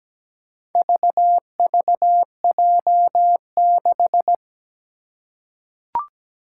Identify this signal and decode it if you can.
Morse code, heard as a keyed tone, about 17 wpm, 700 Hz. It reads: VVJ6
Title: Four Challenging Letter-Number Combinations 17wpm